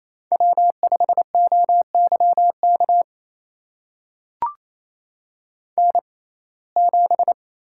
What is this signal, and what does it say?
Morse code, heard as a keyed tone, about 28 wpm, spaced as for 20 wpm, 700 Hz. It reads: W5OYK N7
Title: USA Call Signs - 5 Characters - 6x Character Spacing - 28wpm